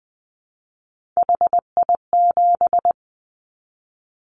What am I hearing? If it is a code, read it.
Morse code, heard as a keyed tone, about 20 wpm, 700 Hz. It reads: HI7
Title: Callsign Prefixes for Top 100 DXCC - 20wpm